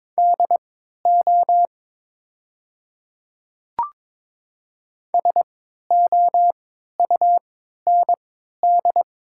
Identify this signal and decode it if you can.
Morse code, heard as a keyed tone, about 22 wpm, spaced as for 12 wpm, 700 Hz. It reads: DO SOUND
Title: 3x Character Spacing - Top 100 Words - 22wpm